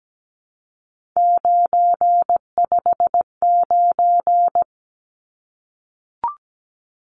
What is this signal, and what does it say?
Morse code, heard as a keyed tone, about 17 wpm, 700 Hz. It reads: 959
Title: Numbers - 3-Digit 17wpm